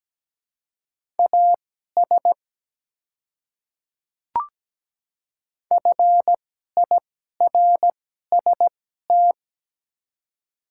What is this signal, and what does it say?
Morse code, heard as a keyed tone, about 17 wpm, spaced as for 12 wpm, 700 Hz. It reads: AS FIRST
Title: 2x Character Spacing - Top 100 Words - 17wpm